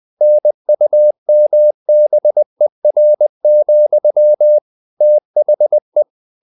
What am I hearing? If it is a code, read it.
Morse code, heard as a keyed tone, about 20 wpm, 600 Hz. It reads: NUMBER, THE
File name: the old man and the sea / the old man and the sea part 1 20wpm